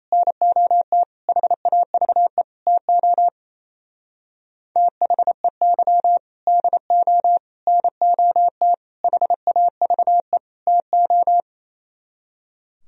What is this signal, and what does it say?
Morse code, heard as a keyed tone, about 35 wpm, 700 Hz. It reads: NOT HAVE TO THEY DO NOT HAVE TO
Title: Speed-Racing - Sets of 5 Words 28wpm